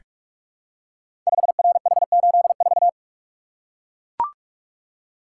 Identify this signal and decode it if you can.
Morse code, heard as a keyed tone, about 44 wpm, 700 Hz. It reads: 5RH84
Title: Code Groups - 5 Letter:Numbers - 45wpm